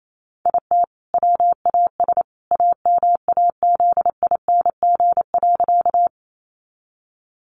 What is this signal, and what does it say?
Morse code, heard as a keyed tone, about 28 wpm, 700 Hz. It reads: IT WAS AMAZING.
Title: Sentences from Top 1500 Words 28wpm